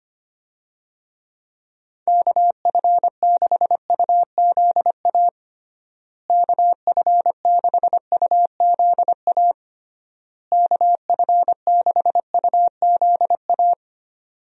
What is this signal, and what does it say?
Morse code, heard as a keyed tone, about 25 wpm, 700 Hz. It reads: KF6UZA KF6UZA KF6UZA
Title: USA Call Signs - 6 Characters - Repeated 3x - 25wpm